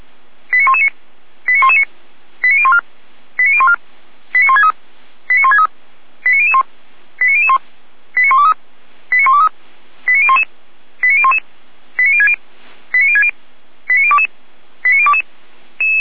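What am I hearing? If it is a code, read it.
Five-tone selective call (ZVEI), heard as quick runs of five tones, tones of about 70 ms. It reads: 89119 89119 89014 89014 89163 89163 89001 89001 89123 89123 89011 89011 89060 89060 89022 89022
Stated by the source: Five-tone ZVEI call received by SDR, from the server of Feuerwehr Orlamünde.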